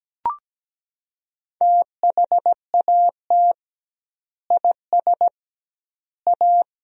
Morse code, heard as a keyed tone, about 17 wpm, 700 Hz. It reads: THAT IS A
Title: 2x Word-Spacing - Sentences from Top 100 Words - 17wpm